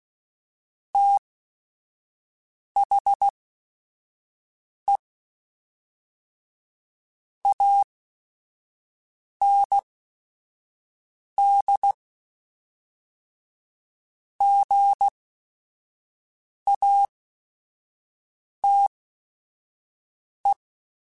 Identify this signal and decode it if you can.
Morse code, heard as a keyed tone, about 16 wpm, spaced as for 5 wpm, 800 Hz. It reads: THE AND GATE
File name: Technical Subjects / Technical Subjects-mp3s / AND Gate @5